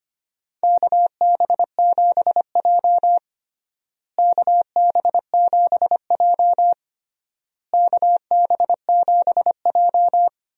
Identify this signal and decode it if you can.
Morse code, heard as a keyed tone, about 25 wpm, 700 Hz. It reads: KB7J KB7J KB7J
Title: USA Call Signs - 4 Characters - Repeated 3x - 25wpm